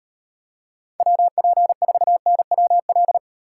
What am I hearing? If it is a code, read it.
Morse code, heard as a keyed tone, about 38 wpm, 700 Hz. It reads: WP4NWL